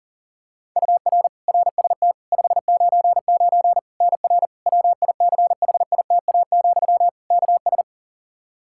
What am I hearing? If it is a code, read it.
Morse code, heard as a keyed tone, about 40 wpm, 700 Hz. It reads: UR RST 599 NR WICHITA, KS
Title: K1USN SST QSO Practice 40wpm